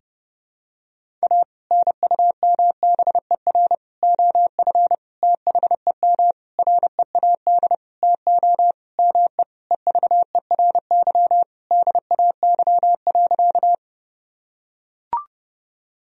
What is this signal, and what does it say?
Morse code, heard as a keyed tone, about 30 wpm, 700 Hz. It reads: A NUMBER OF THEM READ TO ME EVERY DAY.
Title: Sentences from Top 1000 Words - Review Part 1 - 30wpm